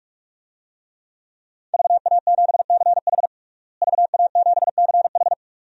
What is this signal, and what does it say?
Morse code, heard as a keyed tone, about 46 wpm, 700 Hz. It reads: VA7CH VA7CH
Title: DXCC-100 Call Signs - 5 Characters - Repeated 2x - 45wpm